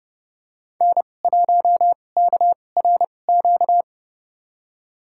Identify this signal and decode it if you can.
Morse code, heard as a keyed tone, about 30 wpm, spaced as for 22 wpm, 700 Hz. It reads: N1KRQ
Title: USA Call Signs - 2x Character Spacing - 30wpm